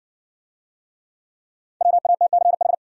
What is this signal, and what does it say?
Morse code, heard as a keyed tone, about 60 wpm, 700 Hz. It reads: WATCH